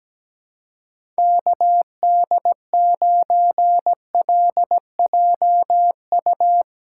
Morse code, heard as a keyed tone, about 17 wpm, 700 Hz. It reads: KD9LJU
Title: USA Callsign DE Callsign 17wpm